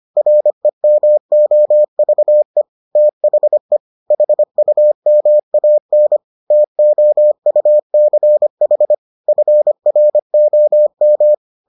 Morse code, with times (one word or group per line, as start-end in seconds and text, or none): 0.2-2.6 REMOVE
3.0-3.8 THE
4.1-6.2 HUMAN
6.5-8.9 TOUCH
9.3-11.3 FROM